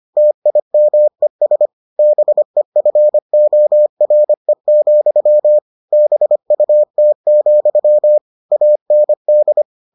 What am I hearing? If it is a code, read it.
Morse code, heard as a keyed tone, about 25 wpm, 600 Hz. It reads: TIMES BEFORE, BUT, AND